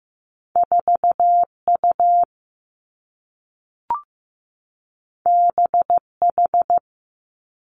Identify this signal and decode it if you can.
Morse code, heard as a keyed tone, about 15 wpm, 700 Hz. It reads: 4U BH
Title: Two Challenging Letter-Number Combinations 15wpm